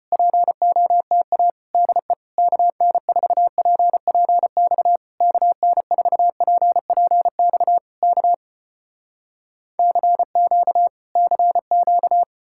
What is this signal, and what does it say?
Morse code, heard as a keyed tone, about 36 wpm, 700 Hz. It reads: POTA DE KN4PPX KN4PPX K CQ CQ